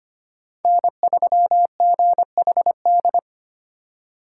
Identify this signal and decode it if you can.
Morse code, heard as a keyed tone, about 25 wpm, 700 Hz. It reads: N3GHD
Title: USA Call Signs - 5 Characters - 2x Character Spacing - 25wpm